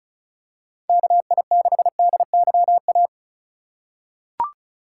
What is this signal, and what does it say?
Morse code, heard as a keyed tone, about 36 wpm, 690 Hz. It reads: KI6DYA
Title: USA Call Signs - 6 Characters - 2x Character Spacing - 35wpm